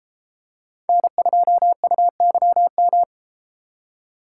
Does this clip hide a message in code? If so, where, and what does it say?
Yes: Morse code from 0.9 to 3.0 s, heard as a keyed tone, about 32 wpm, 700 Hz. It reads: N2UYM